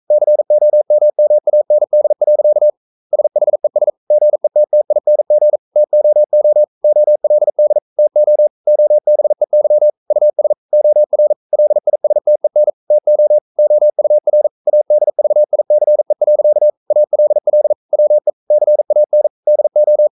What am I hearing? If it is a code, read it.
Morse code, heard as a keyed tone, about 42 wpm, 600 Hz. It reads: COMMAND. SHES GETTING TOO OLD TO OBEY US OR LISTEN TO OUR ADVICE. ALL WE CAN DO
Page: Romanen Never af Ken Follet på engelsk.